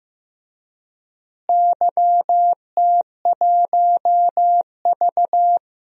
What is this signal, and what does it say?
Morse code, heard as a keyed tone, about 16 wpm, 700 Hz. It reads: YT1V
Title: DXCC-100 Call Signs - 4 Characters - Repeated 2x - 15wpm